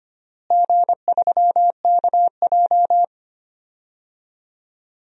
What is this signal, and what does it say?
Morse code, heard as a keyed tone, about 25 wpm, 700 Hz. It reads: G3KJ
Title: DXCC-100 Call Signs - 4 Characters - Repeated 2x - 25wpm